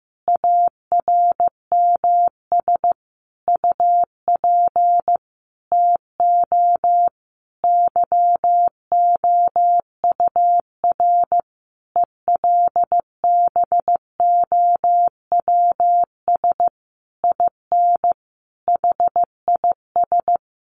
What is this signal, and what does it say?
Morse code, heard as a keyed tone, about 15 wpm, 700 Hz. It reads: ARMS UP TO YOUR ELBOWS IN HIS
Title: A Sound Of Thunder 15wpm